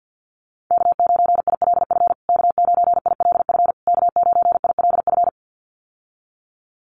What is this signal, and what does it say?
Morse code, heard as a keyed tone, about 50 wpm, 700 Hz. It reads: K9ILF K9ILF K9ILF
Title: USA Call Signs Repeated 3x 50wpm